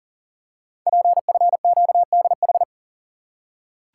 Morse code, heard as a keyed tone, about 40 wpm, 700 Hz. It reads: PFQDH
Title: Code Groups - 5 Letters - 40wpm